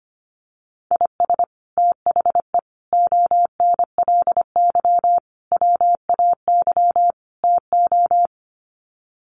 Morse code, heard as a keyed tone, about 25 wpm, 700 Hz. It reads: IS THE ONLY WAY TO
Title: Sets of 5 Words Encore 25wpm